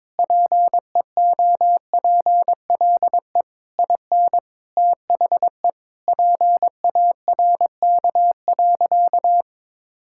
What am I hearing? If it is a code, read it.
Morse code, heard as a keyed tone, about 22 wpm, 700 Hz. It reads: PEOPLE IN THE PARK.
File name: practice / 22wpm / Sentences with 3-Word Phrases - 22wpm